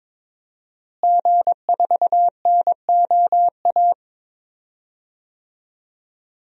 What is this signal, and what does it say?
Morse code, heard as a keyed tone, about 22 wpm, 700 Hz. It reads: G4NOA
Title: DXCC-100 Call Signs - 5 Characters - Repeated 3x - 22wpm